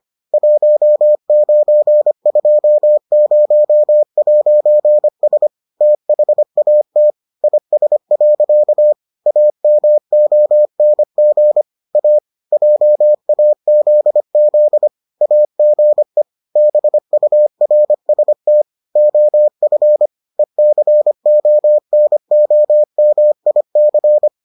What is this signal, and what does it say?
Morse code, heard as a keyed tone, about 25 wpm, 600 Hz. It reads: 1920'S THAT IS. AMONG A JAZZ AGE BURST OF ECONOMIC